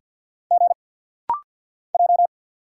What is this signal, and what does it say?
Morse code, heard as a keyed tone, about 50 wpm, 700 Hz. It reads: G J
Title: Single Letters - Rapid-Fire 50wpm